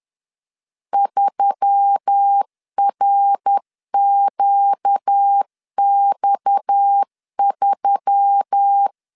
Morse code, heard as a keyed tone, about 10 wpm, 800 Hz. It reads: 3RQX3
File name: LEZIONI / 7 GF9Z / 7 G 10wpm 2501